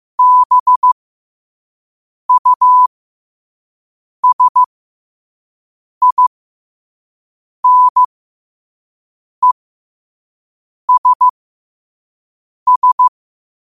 Morse code, heard as a keyed tone, about 15 wpm, spaced as for 5 wpm, 1000 Hz. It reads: BUSINESS
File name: qotdfiles / 5wpm